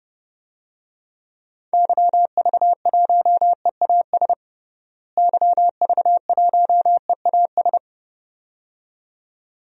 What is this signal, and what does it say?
Morse code, heard as a keyed tone, about 32 wpm, 700 Hz. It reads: YV1EAS YV1EAS
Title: DXCC-100 Call Signs - 6 Characters - Repeated 2x - 30wpm